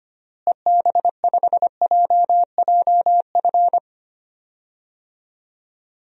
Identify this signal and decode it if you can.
Morse code, heard as a keyed tone, about 25 wpm, 700 Hz. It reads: EB5JJF
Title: DXCC-100 Call Signs - 6 Characters - Repeated 2x - 25wpm